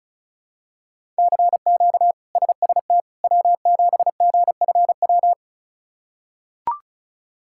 Morse code, heard as a keyed tone, about 34 wpm, 700 Hz. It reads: CQ SST W7GFW
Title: K1USN SST QSO Practice - 35wpm